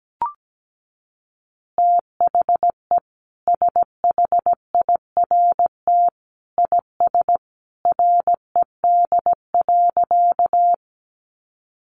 Morse code, heard as a keyed tone, about 17 wpm, 700 Hz. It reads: THE SHIRT IS RED.
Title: Additional Easy Phrases 17wpm